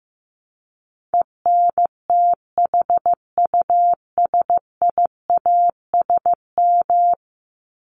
Morse code, heard as a keyed tone, about 15 wpm, 700 Hz. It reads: ENTHUSIASM
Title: Top 1751-2000 Words 15wpm